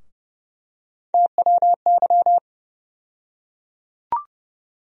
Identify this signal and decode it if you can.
Morse code, heard as a keyed tone, about 30 wpm, 700 Hz. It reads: TWY
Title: USA Call Sign Postfixes - 3 Letters 30wpm